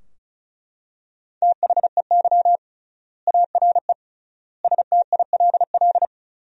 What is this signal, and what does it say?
Morse code, heard as a keyed tone, about 34 wpm, 700 Hz. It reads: THEY ARE STILL
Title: 3x Word-Spacing - Sentences from Top 400 Words - 35wpm